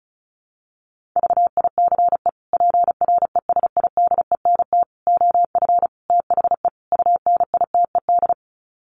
Morse code, heard as a keyed tone, about 34 wpm, 700 Hz. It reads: VICE PRESIDENT OF THE UNITED
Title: Sets of 5 Words Encore2 35wpm